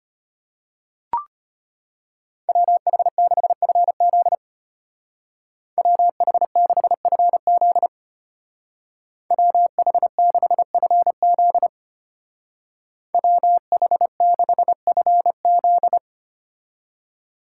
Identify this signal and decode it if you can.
Morse code, heard as a keyed tone, about 32 wpm, 700 Hz. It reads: WH6FZ WH6FZ WH6FZ WH6FZ